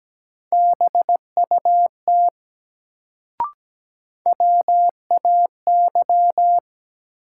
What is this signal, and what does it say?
Morse code, heard as a keyed tone, about 17 wpm, 700 Hz. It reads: BUT WAY
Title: Sets of 2 Words - New Word Drill - 17wpm